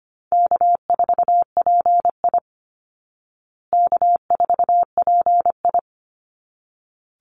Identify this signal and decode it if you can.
Morse code, heard as a keyed tone, about 25 wpm, 700 Hz. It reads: K4PI K4PI
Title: Easy Call Signs Repeated 2x - 25wpm